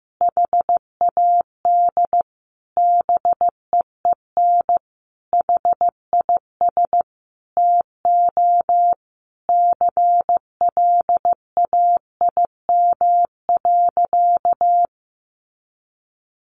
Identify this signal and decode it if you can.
Morse code, heard as a keyed tone, about 15 wpm, 700 Hz. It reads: HAD BEEN HIS TO CLAIM.
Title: Sentences from Top 1500 Words 15wpm